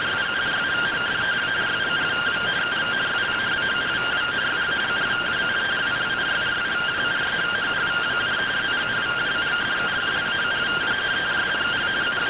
Two-tone teleprinter signal, about 170 Hz shift.
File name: findings / AXIAL fr1 3mm.gen-22 rtty rx